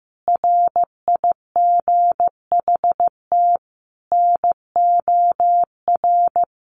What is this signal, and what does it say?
Morse code, heard as a keyed tone, about 15 wpm, 700 Hz. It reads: RIGHT NOR
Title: Sentences from Top 1500 Words 15wpm